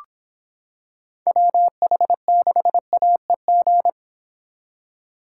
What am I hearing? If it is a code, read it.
Morse code, heard as a keyed tone, about 26 wpm, 700 Hz. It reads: WH6AEG